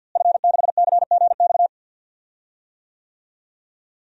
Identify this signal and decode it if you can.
Morse code, heard as a keyed tone, about 50 wpm, 700 Hz. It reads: F6CGX
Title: DXCC-100 Call Signs - 5 Characters - Repeated 3x - 50wpm